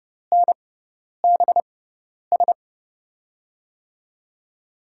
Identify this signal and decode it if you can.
Morse code, heard as a keyed tone, about 31 wpm, spaced as for 10 wpm, 700 Hz. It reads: NBS
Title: USA Call Signs - 5 Characters - 6x Character Spacing - 30wpm